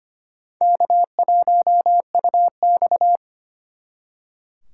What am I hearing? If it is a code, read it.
Morse code, heard as a keyed tone, about 32 wpm, 700 Hz. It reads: K1UX